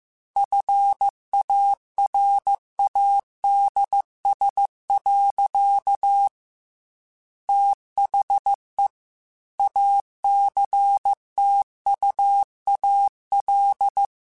Morse code, heard as a keyed tone, about 15 wpm, 800 Hz. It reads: FARADS. THE ACTUAL